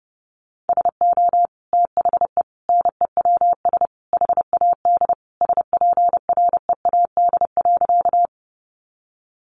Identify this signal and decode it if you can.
Morse code, heard as a keyed tone, about 30 wpm, 700 Hz. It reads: SO THE NEWS HAD SPREAD.
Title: Sentences from Top 1200 Words 30wpm